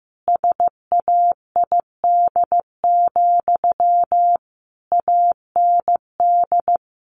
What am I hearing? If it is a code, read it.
Morse code, heard as a keyed tone, about 15 wpm, 700 Hz. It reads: SAID, AND
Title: Sentences from Top 300 Words 15wpm